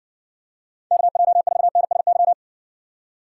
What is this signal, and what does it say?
Morse code, heard as a keyed tone, about 58 wpm, 700 Hz. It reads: KJ4NSY